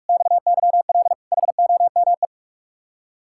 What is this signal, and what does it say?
Morse code, heard as a keyed tone, about 46 wpm, 700 Hz. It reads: XYL HOME
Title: Speed-Racing Common QSO Phrases Encore - 45wpm